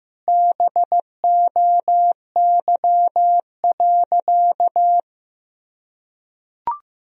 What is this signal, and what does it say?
Morse code, heard as a keyed tone, about 15 wpm, 700 Hz. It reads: BOY.
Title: Sentences from Top 1000 Words - Review Part 3 - 15wpm